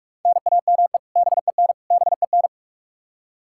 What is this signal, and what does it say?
Morse code, heard as a keyed tone, about 41 wpm, 700 Hz. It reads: NAME BEN BEN
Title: Practice QSO - Exchange 1 - Beginning - 45wpm